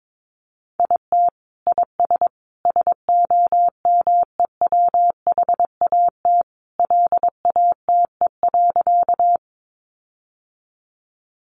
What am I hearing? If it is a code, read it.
Morse code, heard as a keyed tone, about 22 wpm, 700 Hz. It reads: IT IS SOMEWHAT LATE.
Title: Sentences from Top 1300 Words 22wpm